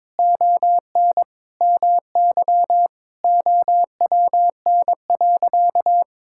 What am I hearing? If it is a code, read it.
Morse code, heard as a keyed tone, about 22 wpm, 700 Hz. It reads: ON MY OWN.
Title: Sentences from Top 1000 Words - Review Part 6 - 22wpm